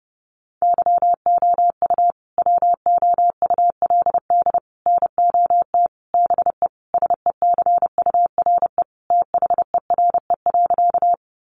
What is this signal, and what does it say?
Morse code, heard as a keyed tone, about 30 wpm, 700 Hz. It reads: YOU WOULD NOT BE SECURE THERE.
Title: Sentences from Top 1200 Words 30wpm